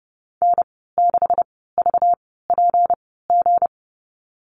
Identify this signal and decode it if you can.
Morse code, heard as a keyed tone, about 30 wpm, spaced as for 17 wpm, 700 Hz. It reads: N6VPG
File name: Lesson 37g - Letter x - Call Signs 30wpm